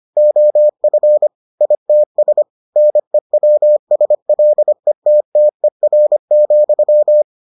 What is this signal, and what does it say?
Morse code, heard as a keyed tone, about 25 wpm, 600 Hz. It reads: OF ITS NEWSLETTER,